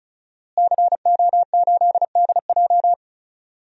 Morse code, heard as a keyed tone, about 36 wpm, 700 Hz. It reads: CO8DJ
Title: DXCC-100 Call Signs - 5 Characters - Repeated 3x - 35wpm